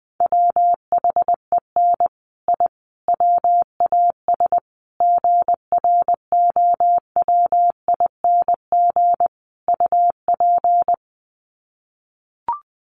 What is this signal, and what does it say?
Morse code, heard as a keyed tone, about 20 wpm, 700 Hz. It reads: WHEN I WAS GROWING UP
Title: Sets of 5 Words Encore 20wpm